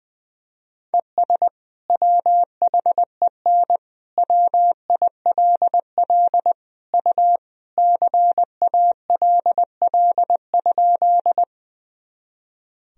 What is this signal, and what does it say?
Morse code, heard as a keyed tone, about 20 wpm, 700 Hz. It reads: ES WHEN WILL U CALL?
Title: Sentences from Top 100 Words and Abbreviations 20wpm